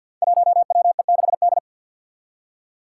Morse code, heard as a keyed tone, about 48 wpm, 700 Hz. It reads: 1WE6D
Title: Five Challenging Letter-Number Combinations 50wpm